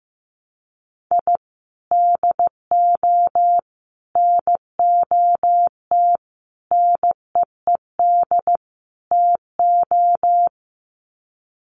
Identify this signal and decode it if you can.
Morse code, heard as a keyed tone, about 15 wpm, 700 Hz. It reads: I DO NOT NEED TO
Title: Sets of 5 Words 15wpm